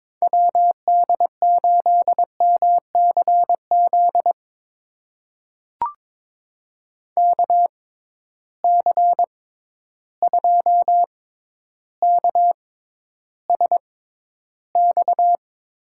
Morse code, heard as a keyed tone, about 22 wpm, spaced as for 11 wpm, 700 Hz. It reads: WD8MCZ KC2KSX
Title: USA Call Signs - 6 Characters - 6x Character Spacing - 22wpm